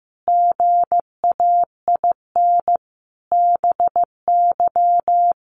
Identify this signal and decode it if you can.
Morse code, heard as a keyed tone, about 15 wpm, 700 Hz. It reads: GAIN BY
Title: Sentences from Top 1100 Words 15wpm